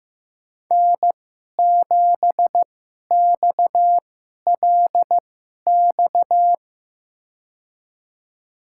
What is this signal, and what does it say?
Morse code, heard as a keyed tone, about 15 wpm, spaced as for 11 wpm, 700 Hz. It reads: N7XLX
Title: USA Call Signs - 5 Characters - 2x Character Spacing - 15wpm